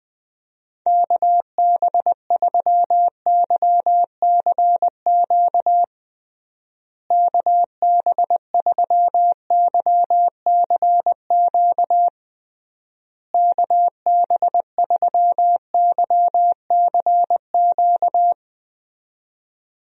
Morse code, heard as a keyed tone, about 20 wpm, 700 Hz. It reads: KB3YCQ KB3YCQ KB3YCQ
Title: USA Call Signs - 6 Characters - Repeated 3x - 20wpm